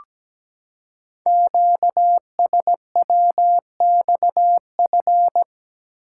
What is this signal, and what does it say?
Morse code, heard as a keyed tone, about 17 wpm, 700 Hz. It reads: QSWXF
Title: Code Groups - 5 Letters - 17wpm